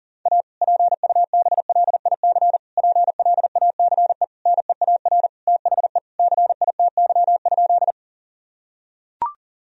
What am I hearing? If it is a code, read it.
Morse code, heard as a keyed tone, about 37 wpm, 700 Hz. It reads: A PUBLIC PLACE NEAR THE CITY?
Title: Sentences from Top 1000 Words - Review Part 3 - 40wpm